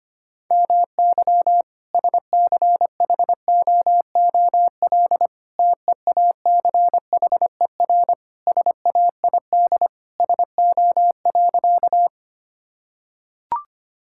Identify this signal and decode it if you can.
Morse code, heard as a keyed tone, about 25 wpm, 700 Hz. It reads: MY SCHOOL TEACHER SAID SO.
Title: Sentences from Top 1000 Words - Review Part 6 - 25wpm